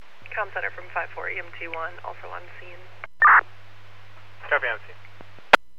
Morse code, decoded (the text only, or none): none